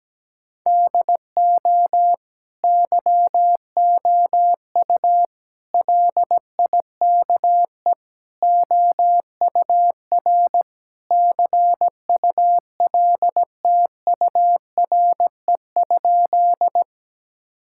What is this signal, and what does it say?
Morse code, heard as a keyed tone, about 17 wpm, 700 Hz. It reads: DO YOU LIKE OUR CULTURE?
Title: Sentences from Top 1000 Words - Review Part 6 - 17wpm